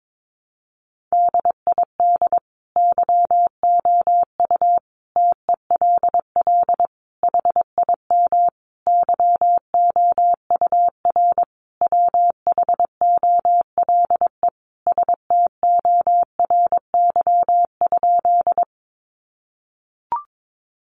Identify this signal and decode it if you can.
Morse code, heard as a keyed tone, about 22 wpm, 700 Hz. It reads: DID YOU TELL HIM YOUR WHOLE STORY?
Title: Sentences from Top 400 Words 22wpm